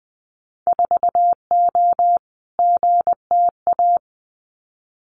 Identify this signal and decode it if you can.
Morse code, heard as a keyed tone, about 20 wpm, 700 Hz. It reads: 4O GTA
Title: Winter Field Day Exchanges 20wpm